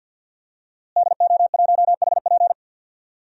Morse code, heard as a keyed tone, about 52 wpm, 700 Hz. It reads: DO1HP
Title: DXCC-100 Call Signs - 5 Characters - Repeated 2x - 50wpm